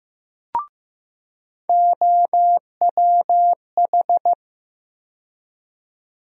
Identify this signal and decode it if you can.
Morse code, heard as a keyed tone, about 15 wpm, 700 Hz. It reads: OWH
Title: USA Call Sign Postfixes - 3 Letters 15wpm